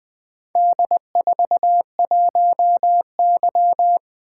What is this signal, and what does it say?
Morse code, heard as a keyed tone, about 20 wpm, 700 Hz. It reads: D41Y
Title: Four Letter-Number Combinations - 20wpm